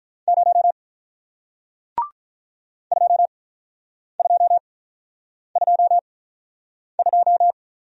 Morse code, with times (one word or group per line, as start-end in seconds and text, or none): 0.3-0.7 0
2.9-7.5 2222